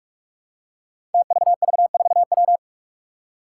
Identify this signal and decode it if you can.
Morse code, heard as a keyed tone, about 44 wpm, 700 Hz. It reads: TVV4W
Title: Five Challenging Letter-Number Combinations 45wpm